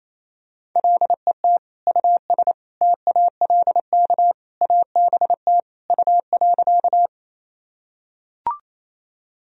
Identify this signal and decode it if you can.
Morse code, heard as a keyed tone, about 28 wpm, 700 Hz. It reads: LET US TALK ABT U.
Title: Sentences from Top 400 Words and Abbreviations 28wpm